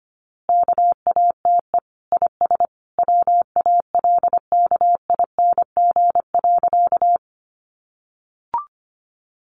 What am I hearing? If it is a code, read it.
Morse code, heard as a keyed tone, about 25 wpm, 700 Hz. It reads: KATE IS WALKING.
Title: Easy Phrases 25wpm